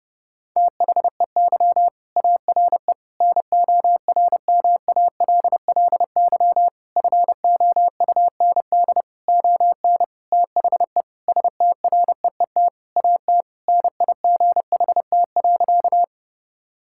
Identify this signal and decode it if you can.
Morse code, heard as a keyed tone, about 32 wpm, 700 Hz. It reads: THEY ARE NORMALLY FOUND ON THE STREET AT NIGHT.
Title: Sentences from Top 1000 Words - Review Part 6 - 30wpm